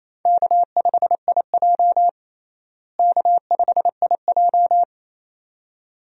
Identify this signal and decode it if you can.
Morse code, heard as a keyed tone, about 28 wpm, 700 Hz. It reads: K5IJ K5IJ
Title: USA Call Signs - 4 Characters - Repeated 2x - 28wpm